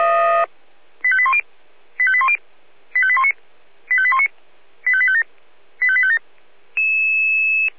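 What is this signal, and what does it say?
Five-tone selective call (ZVEI), heard as quick runs of five tones, tones of about 70 ms. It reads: 85810 85810 85819 85819 85857 85857
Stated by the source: ZVEI five-tone selective call received by SDR.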